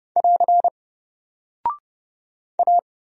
Morse code, heard as a keyed tone, about 30 wpm, 700 Hz. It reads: <AR> A
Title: Single Character (Letter:Number:Punctuation:Prosign) 30wpm